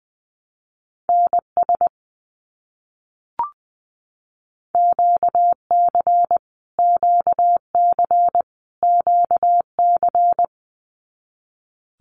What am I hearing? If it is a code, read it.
Morse code, heard as a keyed tone, about 19 wpm, 700 Hz. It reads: NS QC QC QC
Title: Canadian Provinces and Territories - Repeated 3x - 20wpm